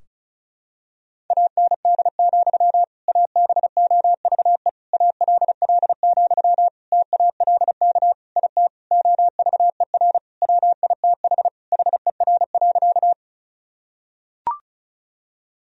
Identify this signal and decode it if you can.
Morse code, heard as a keyed tone, about 36 wpm, 700 Hz. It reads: AND, ABOVE ALL, TALK IT OVER WITH HER.
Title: Sentences from Top 1000 Words - Review Part 2 - 35wpm